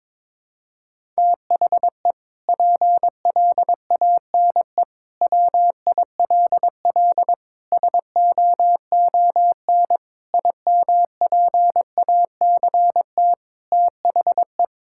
Morse code, heard as a keyed tone, about 22 wpm, 700 Hz. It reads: THE PLANE WILL SOON IMPACT THE
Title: Sentences from Top 1000 Words - Review Part 6 - 22wpm